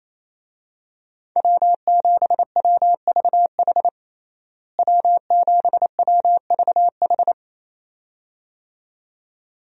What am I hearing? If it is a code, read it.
Morse code, heard as a keyed tone, about 27 wpm, 700 Hz. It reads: W7WVH W7WVH
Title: USA Call Signs - 5 Characters - Repeated 2x - 28wpm